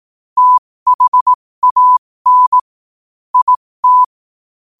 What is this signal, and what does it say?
Morse code, heard as a keyed tone, about 18 wpm, spaced as for 15 wpm, 1000 Hz. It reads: THAN IT
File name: qotdfiles / 15wpm